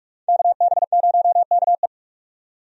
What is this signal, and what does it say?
Morse code, heard as a keyed tone, about 41 wpm, 700 Hz. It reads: KB0KE
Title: USA Call Signs - 5 Characters - 45wpm